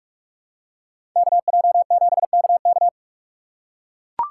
Morse code, heard as a keyed tone, about 44 wpm, 700 Hz. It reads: KJ7KK